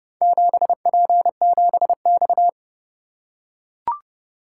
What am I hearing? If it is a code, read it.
Morse code, heard as a keyed tone, about 27 wpm, 700 Hz. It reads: 7P7X